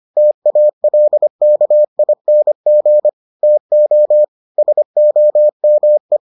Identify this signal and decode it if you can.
Morse code, heard as a keyed tone, about 25 wpm, 600 Hz. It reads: TALKING TO SOME